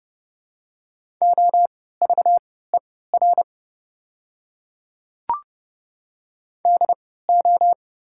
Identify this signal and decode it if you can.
Morse code, heard as a keyed tone, about 30 wpm, spaced as for 17 wpm, 700 Hz. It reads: OVER DO
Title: 3x Character Spacing - Top 100 Words - 30wpm